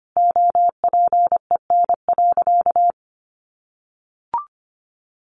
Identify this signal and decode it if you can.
Morse code, heard as a keyed tone, about 25 wpm, 700 Hz. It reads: OPEN.
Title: Sentences from Top 1300 Words 25wpm